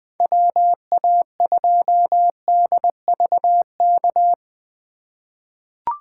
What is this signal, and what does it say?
Morse code, heard as a keyed tone, about 20 wpm, 700 Hz. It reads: WA2DVK
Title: USA Call Signs - 6 Characters - 3x Character Spacing - 20wpm